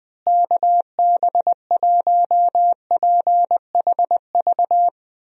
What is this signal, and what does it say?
Morse code, heard as a keyed tone, about 20 wpm, 700 Hz. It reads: KB1PHV
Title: USA Call Signs - 6 Characters - Repeated 2x - 20wpm